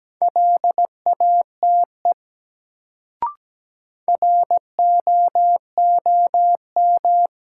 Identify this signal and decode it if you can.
Morse code, heard as a keyed tone, about 17 wpm, 700 Hz. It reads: LATE ROOM
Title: Sentences from Top 300 Words - New Word Drill - 17wpm